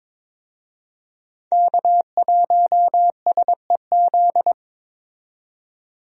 Morse code, heard as a keyed tone, about 22 wpm, 700 Hz. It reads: K1SEZ
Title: USA Call Signs - 5 Characters - 22wpm